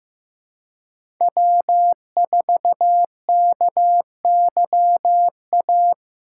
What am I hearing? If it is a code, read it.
Morse code, heard as a keyed tone, about 16 wpm, 700 Hz. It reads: W4KYA
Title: USA Call Signs - 2x Character Spacing - 15wpm